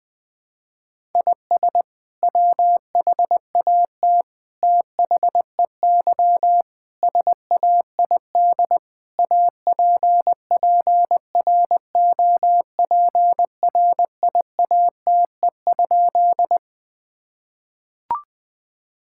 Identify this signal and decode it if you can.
Morse code, heard as a keyed tone, about 20 wpm, 700 Hz. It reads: IS WHAT THEY SAID APPROPRIATE?
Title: Sentences from Top 1000 Words - Review Part 5 - 20wpm